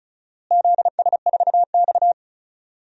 Morse code, heard as a keyed tone, about 34 wpm, 700 Hz. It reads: ZS4X
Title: DXCC-100 Call Signs - 4 Characters - Repeated 2x - 35wpm